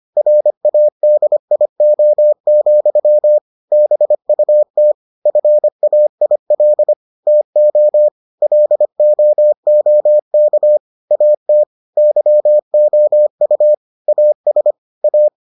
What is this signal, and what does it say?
Morse code, heard as a keyed tone, about 25 wpm, 600 Hz. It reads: RADIO, BUT FAIL TO LOOK AT YOU AS A